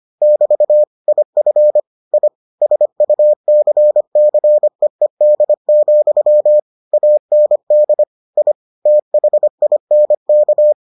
Morse code, heard as a keyed tone, about 25 wpm, 600 Hz. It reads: <BT> IF I SUCCEED, AND I THINK